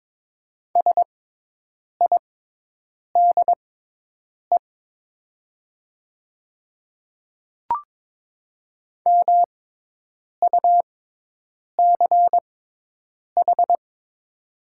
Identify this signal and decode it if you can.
Morse code, heard as a keyed tone, about 22 wpm, spaced as for 8 wpm, 700 Hz. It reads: SIDE MUCH
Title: 6x Character Spacing - Top 300 Words - 22wpm